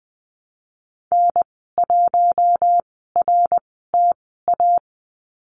Morse code, heard as a keyed tone, about 20 wpm, spaced as for 15 wpm, 700 Hz. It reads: N1RTA
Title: 055 Lesson 10g - Letter r - Call Signs 20wpm